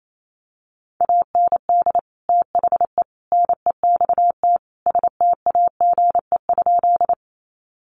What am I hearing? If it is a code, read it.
Morse code, heard as a keyed tone, about 28 wpm, 700 Hz. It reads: AND THE NEXT STAGE?
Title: Sentences from Top 1200 Words 28wpm